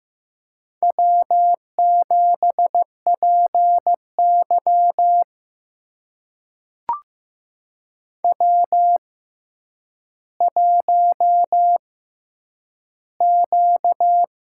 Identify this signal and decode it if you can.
Morse code, heard as a keyed tone, about 15 wpm, spaced as for 9 wpm, 700 Hz. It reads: W7PY W1Q
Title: USA Call Signs - 4 Characters - 6x Character Spacing - 15wpm